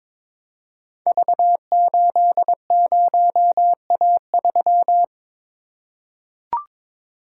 Morse code, heard as a keyed tone, about 23 wpm, 700 Hz. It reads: V80A3
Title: Code Groups - 5 Letter:Numbers - 22wpm